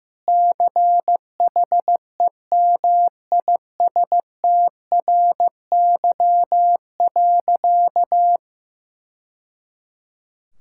Morse code, heard as a keyed tone, about 15 wpm, 700 Hz. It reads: CHEMISTRY.
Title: Sentences from Top 1000 Words - Review Part 6 - 15wpm